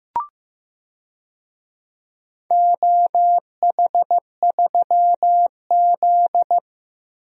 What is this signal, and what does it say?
Morse code, heard as a keyed tone, about 15 wpm, 700 Hz. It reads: OH3Z
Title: DXCC-100 Call Signs - 4 Characters - Repeated 2x - 15wpm